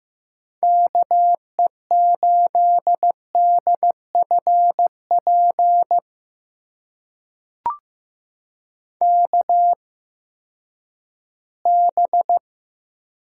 Morse code, heard as a keyed tone, about 15 wpm, spaced as for 10 wpm, 700 Hz. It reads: KE8DFP KB
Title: USA Call Signs - 6 Characters - 8x Character Spacing - 15wpm